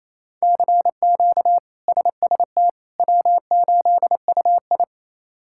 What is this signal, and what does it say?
Morse code, heard as a keyed tone, about 27 wpm, 700 Hz. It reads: CQ SST W8UI
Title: K1USN SST QSO Practice - 28wpm